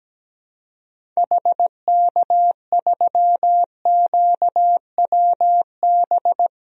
Morse code, heard as a keyed tone, about 17 wpm, 700 Hz. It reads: HK3QWB
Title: DXCC-100 Call Signs - 6 Characters - Repeated 2x - 17wpm